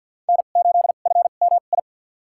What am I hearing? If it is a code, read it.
Morse code, heard as a keyed tone, about 52 wpm, spaced as for 36 wpm, 700 Hz. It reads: N8FMI
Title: USA Call Signs - 5 Characters - 2x Character Spacing - 50wpm